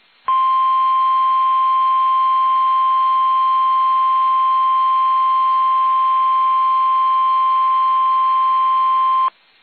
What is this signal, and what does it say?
Emergency Alert System (SAME) tones (the 1050 Hz attention tone). It is heard from 0.2 to 9.4 s.